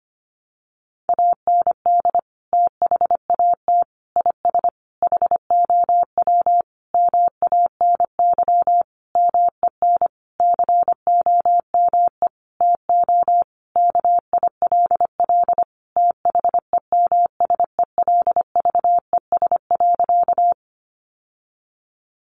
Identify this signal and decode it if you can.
Morse code, heard as a keyed tone, about 25 wpm, 700 Hz. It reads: AND THAT IS HOW MANY MEN COME TO KILL THEMSELVES.
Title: Sentences from Top 1400 Words 25wpm